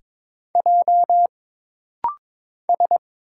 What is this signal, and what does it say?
Morse code, heard as a keyed tone, about 22 wpm, 700 Hz. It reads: J S
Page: This rapid-fire timing is geared to push your ICR (Instant Character Recognition) skills.